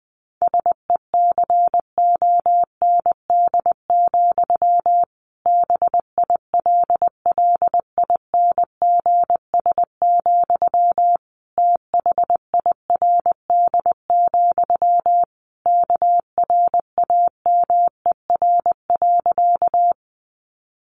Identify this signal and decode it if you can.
Morse code, heard as a keyed tone, about 20 wpm, 700 Hz. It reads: SECOND, BILLINGS, THIRD, KRAMER.
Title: A Sound Of Thunder 20wpm